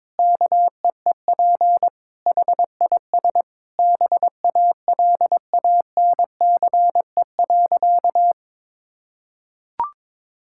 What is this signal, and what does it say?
Morse code, heard as a keyed tone, about 22 wpm, 700 Hz. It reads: KEEP HIS BALANCE.
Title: Sentences from Top 1000 Words - Review Part 6 - 22wpm